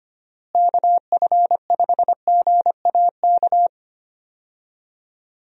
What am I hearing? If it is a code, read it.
Morse code, heard as a keyed tone, about 25 wpm, 700 Hz. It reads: KF5GAK
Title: USA Call Signs - 6 Characters - Repeated 3x - 25wpm